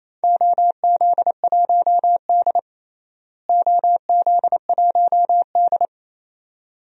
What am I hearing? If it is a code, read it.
Morse code, heard as a keyed tone, about 28 wpm, 700 Hz. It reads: OZ1D OZ1D
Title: DXCC-100 Call Signs - 4 Characters - Repeated 2x - 28wpm